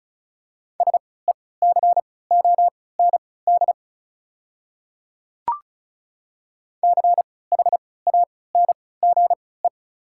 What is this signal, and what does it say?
Morse code, heard as a keyed tone, about 34 wpm, spaced as for 20 wpm, 690 Hz. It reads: SECOND CHANGE
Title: 3x Character Spacing - Top 500 Words - 35wpm